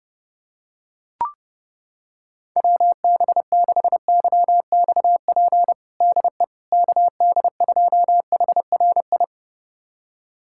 Morse code, heard as a keyed tone, about 30 wpm, 700 Hz. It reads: WB6YXP DE KD2HRI
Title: USA Callsign DE Callsign 30wpm